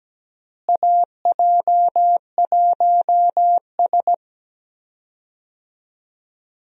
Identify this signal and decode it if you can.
Morse code, heard as a keyed tone, about 17 wpm, 700 Hz. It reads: AJ1S
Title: USA Call Signs - 4 Characters - Repeated 2x - 17wpm